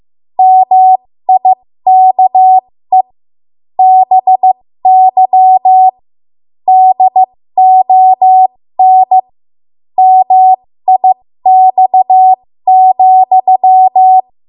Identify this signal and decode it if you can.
Morse code, heard as a keyed tone, about 15 wpm, spaced as for 13 wpm, 750 Hz. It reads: MIKE BY DON MIX,
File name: Morse / 13WPM